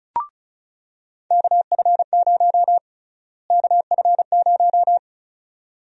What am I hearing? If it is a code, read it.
Morse code, heard as a keyed tone, about 36 wpm, 700 Hz. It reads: KF0 KF0
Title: USA Call Sign Prefixes - Repeated 2x - 35wpm